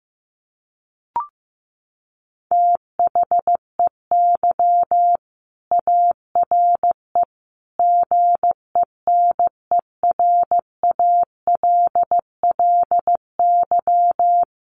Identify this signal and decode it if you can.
Morse code, heard as a keyed tone, about 15 wpm, 700 Hz. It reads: THEY ARE GENERALLY
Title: Sentences from Top 600 Words 15wpm